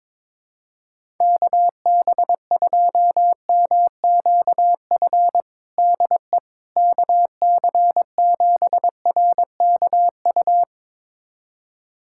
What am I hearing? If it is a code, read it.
Morse code, heard as a keyed tone, about 22 wpm, 700 Hz. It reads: KB2MQF DE KC7RKU
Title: USA Callsign DE Callsign 22wpm